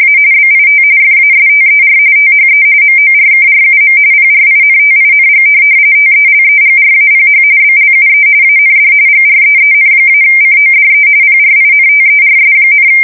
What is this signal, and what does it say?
RTTY, about 45.45 baud. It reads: A BIT FASTER THE MORKRUM PRINTING TELEGRAPH BLUE AND GREEN-CODE PRINTERS,
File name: fsk170 / teletype-corp